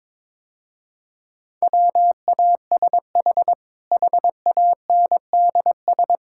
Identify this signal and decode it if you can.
Morse code, heard as a keyed tone, about 22 wpm, 700 Hz. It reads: WASH HANDS
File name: practice / 22wpm / Easy Sentences with Hints - 22wpm